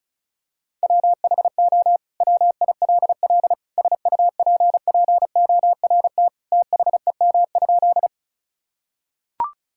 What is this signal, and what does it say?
Morse code, heard as a keyed tone, about 36 wpm, 700 Hz. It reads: WHO WILL SUPPORT THEM?
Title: Sentences from Top 1000 Words - Review Part 5 - 35wpm